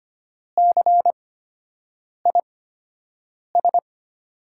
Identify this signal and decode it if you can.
Morse code, heard as a keyed tone, about 25 wpm, spaced as for 7 wpm, 700 Hz. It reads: CIS
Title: USA Call Signs - 8x Character Spacing - 25wpm